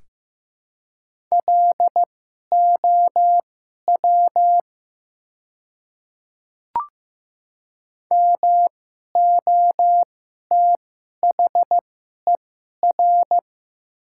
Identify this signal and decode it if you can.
Morse code, heard as a keyed tone, about 15 wpm, spaced as for 11 wpm, 700 Hz. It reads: LOW MOTHER
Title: 2x Character Spacing - Top 200 Words - 15wpm